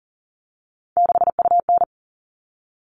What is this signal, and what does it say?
Morse code, heard as a keyed tone, about 40 wpm, 700 Hz. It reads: 6UN
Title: All Cut Numbers - 3-Digit 40wpm